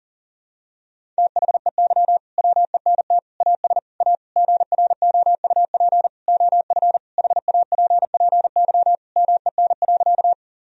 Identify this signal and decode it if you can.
Morse code, heard as a keyed tone, about 37 wpm, 700 Hz. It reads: THEY WENT AS A GROUP OF HAPPY MEN.
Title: Sentences from Top 1000 Words - Review Part 1 - 40wpm